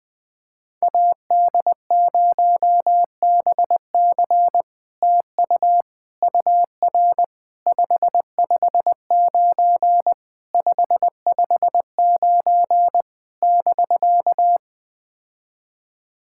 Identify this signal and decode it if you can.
Morse code, heard as a keyed tone, about 21 wpm, 700 Hz. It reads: AD0BC TU UR 559 559 <BK>
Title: POTA QSO Practice - 20wpm